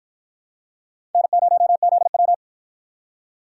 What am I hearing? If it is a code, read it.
Morse code, heard as a keyed tone, about 53 wpm, 700 Hz. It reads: N0ZW